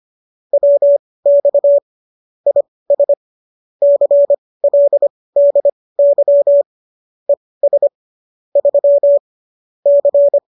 Morse code, heard as a keyed tone, about 25 wpm, spaced as for 18 wpm, 570 Hz. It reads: WX IS CLDY ES 3 C
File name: qso218 18